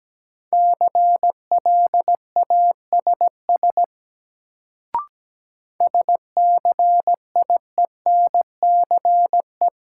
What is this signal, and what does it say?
Morse code, heard as a keyed tone, about 17 wpm, 700 Hz. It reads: CLASS SCIENCE
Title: Sentences from Top 1400 Words - New Word Drill Alt - 17wpm